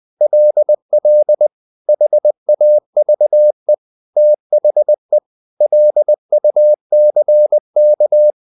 Morse code, heard as a keyed tone, about 20 wpm, 600 Hz. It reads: LL HAVE THE LUCK
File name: the old man and the sea part 2 20wpm